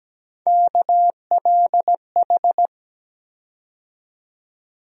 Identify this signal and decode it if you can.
Morse code, heard as a keyed tone, about 17 wpm, 700 Hz. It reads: KLH
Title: USA Call Sign Postfixes - 3 Letters 17wpm